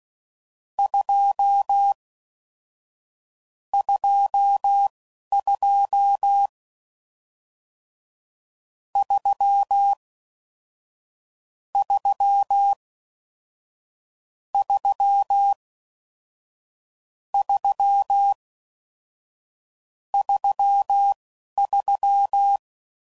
Morse code, heard as a keyed tone, about 16 wpm, spaced as for 11 wpm, 800 Hz. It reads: 2 22 3 3 3 3 33